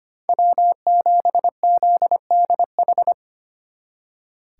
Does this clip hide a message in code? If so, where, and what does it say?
Yes: Morse code from 0.3 to 3.1 s, heard as a keyed tone, about 33 wpm, 700 Hz. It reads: W7ZDH